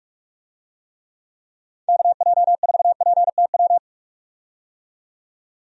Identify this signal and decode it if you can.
Morse code, heard as a keyed tone, about 46 wpm, 700 Hz. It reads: KJ4PTW
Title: USA Call Signs - 6 Characters - 45wpm